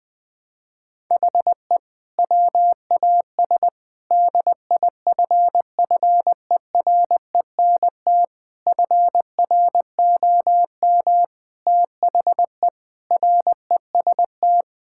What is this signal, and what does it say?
Morse code, heard as a keyed tone, about 20 wpm, 700 Hz. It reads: HE WAS DIFFERENT FROM THE REST
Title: Sentences with 5-Word Phrases - Spoken Beginning - 20wpm